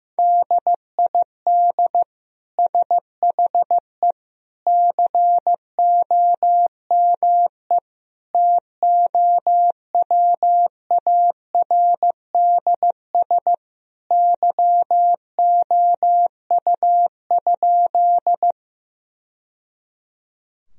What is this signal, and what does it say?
Morse code, heard as a keyed tone, about 16 wpm, 700 Hz. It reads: DID SHE COME TOWARDS YOU?
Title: Sentences from Top 1000 Words - Review Part 2 - 15wpm